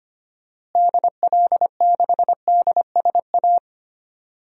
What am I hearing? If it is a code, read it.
Morse code, heard as a keyed tone, about 25 wpm, 700 Hz. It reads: DL6DSA
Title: DXCC-100 Call Signs - 6 Characters - Repeated 2x - 25wpm